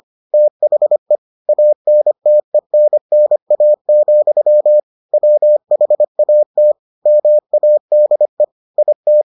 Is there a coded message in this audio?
Yes: Morse code from 0.3 to 9.2 s, heard as a keyed tone, about 25 wpm, 600 Hz. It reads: THE ANTENNA, WHAT MADE IT